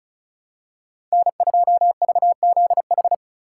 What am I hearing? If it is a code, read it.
Morse code, heard as a keyed tone, about 35 wpm, 700 Hz. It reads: N2VZH
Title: USA Call Signs - 4x Character Spacing - 35wpm